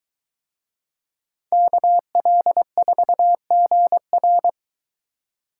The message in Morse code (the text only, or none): KL4GR